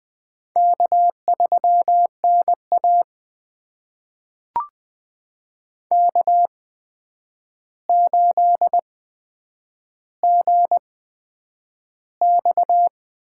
Morse code, heard as a keyed tone, about 20 wpm, spaced as for 9 wpm, 700 Hz. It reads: K3NA K8GX
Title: USA Call Signs - 8x Character Spacing - 20wpm